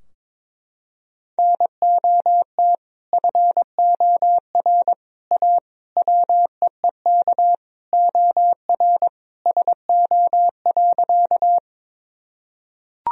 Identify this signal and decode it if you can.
Morse code, heard as a keyed tone, about 22 wpm, 700 Hz. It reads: NOT FOR A WEEK OR SO.
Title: Sentences from Top 1000 Words - Review Part 2 - 22wpm